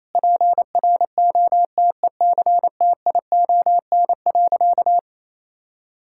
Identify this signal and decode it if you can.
Morse code, heard as a keyed tone, about 28 wpm, 700 Hz. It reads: PROTECTION.
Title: Sentences from Top 1600 Words 28wpm